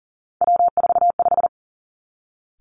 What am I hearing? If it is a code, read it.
Morse code, heard as a keyed tone, about 42 wpm, 700 Hz. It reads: W45
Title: Three Challenging Letter-Number Combinations 40wpm